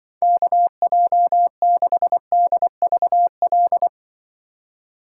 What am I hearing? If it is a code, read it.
Morse code, heard as a keyed tone, about 24 wpm, 700 Hz. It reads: KJ6DVL
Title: Speed Racing - USA Call Signs 20wpm